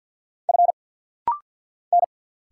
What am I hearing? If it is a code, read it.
Morse code, heard as a keyed tone, about 50 wpm, 700 Hz. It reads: F N
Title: Single Letters - Rapid-Fire 50wpm